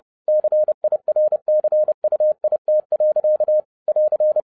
Morse code, heard as a keyed tone, about 30 wpm, 610 Hz. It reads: CIRCUIT. <AR>
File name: 2011-07-mp3 / n20 GREETINGS FROM SINGAPORE 30wpm